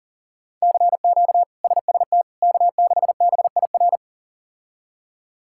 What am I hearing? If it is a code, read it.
Morse code, heard as a keyed tone, about 40 wpm, 700 Hz. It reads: CQ SST K6BIR
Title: K1USN SST QSO Practice - 40wpm